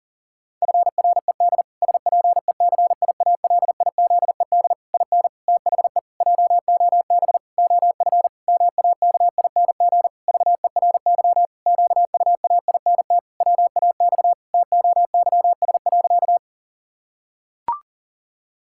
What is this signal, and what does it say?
Morse code, heard as a keyed tone, about 37 wpm, 700 Hz. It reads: FRED SPECIALIZED IN THE JOB OF MAKING VERY QUAINT WAX TOYS.
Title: Pangram Challenge v2 - 40wpm